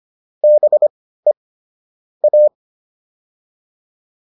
Morse code, heard as a keyed tone, about 25 wpm, spaced as for 15 wpm, 600 Hz. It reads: BE A
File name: WD404 15